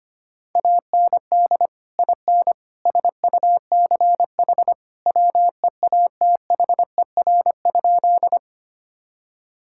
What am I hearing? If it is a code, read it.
Morse code, heard as a keyed tone, about 25 wpm, 700 Hz. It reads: AND IN SUCH WEATHER?
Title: Sentences from Top 1000 Words - Review Part 5 - 25wpm